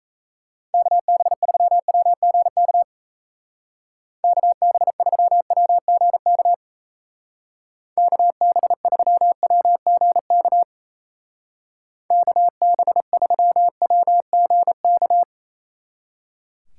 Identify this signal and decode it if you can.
Morse code, heard as a keyed tone, about 36 wpm, 700 Hz. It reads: KB3WGK KB3WGK KB3WGK KB3WGK